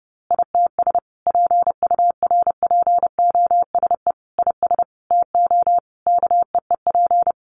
Morse code, heard as a keyed tone, about 30 wpm, 700 Hz. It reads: ITS PURPOSE IS TO KEEP
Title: A Sound Of Thunder 30wpm